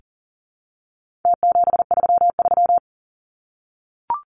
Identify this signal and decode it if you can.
Morse code, heard as a keyed tone, about 40 wpm, 700 Hz. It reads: T733
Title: Numbers - Contest Style - 4-Digit 40wpm